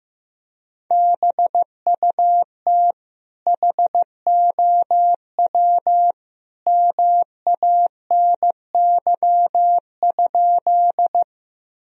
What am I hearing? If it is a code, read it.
Morse code, heard as a keyed tone, about 15 wpm, 700 Hz. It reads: BUT HOW MANY?
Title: Sentences from Top 1000 Words - Review Part 1 - 15wpm